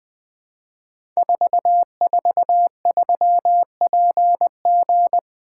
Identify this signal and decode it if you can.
Morse code, heard as a keyed tone, about 20 wpm, 700 Hz. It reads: 443PG
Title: Code Groups - 5 Letter:Numbers - 20wpm